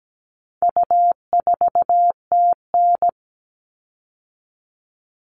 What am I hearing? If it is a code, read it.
Morse code, heard as a keyed tone, about 17 wpm, 700 Hz. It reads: U4TN
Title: All Cut Numbers - 4-Digit 17wpm